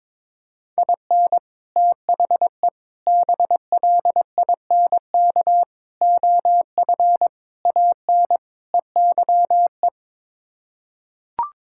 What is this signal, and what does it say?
Morse code, heard as a keyed tone, about 22 wpm, 700 Hz. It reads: IN THE BLINK OF AN EYE
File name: practice / 22wpm / Sets of 6 Words - 22wpm